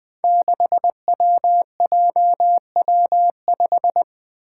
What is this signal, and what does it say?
Morse code, heard as a keyed tone, about 21 wpm, 700 Hz. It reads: 6WJW5
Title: Five Challenging Letter-Number Combinations 20wpm